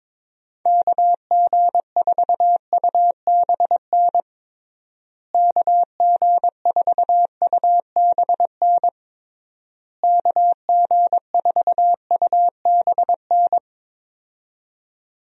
Morse code, heard as a keyed tone, about 22 wpm, 700 Hz. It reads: KG4UBN KG4UBN KG4UBN
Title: USA Call Signs - 6 Characters - Repeated 3x - 22wpm